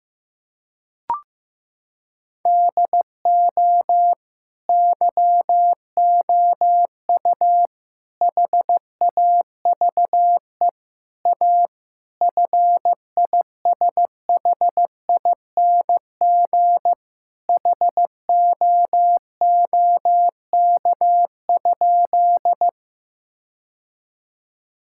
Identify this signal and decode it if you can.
Morse code, heard as a keyed tone, about 15 wpm, 700 Hz. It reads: DO YOU HAVE A FISHING HOOK?
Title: Sentences from Top 1600 Words 15wpm